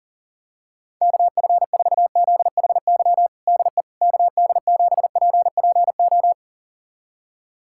Morse code, heard as a keyed tone, about 40 wpm, 700 Hz. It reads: KF4ZHY DE KD7PPO
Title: USA Callsign DE Callsign 40wpm